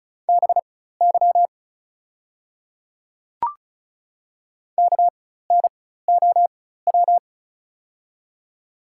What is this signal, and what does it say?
Morse code, heard as a keyed tone, about 34 wpm, spaced as for 16 wpm, 700 Hz. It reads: BY KNOW
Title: 4x Character Spacing - Top 200 Words - 35wpm